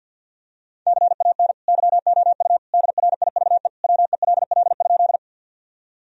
Morse code, heard as a keyed tone, about 51 wpm, 700 Hz. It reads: CAN YOU DRIVE WELL?
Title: Sentences from Top 1000 Words - Review Part 2 - 50wpm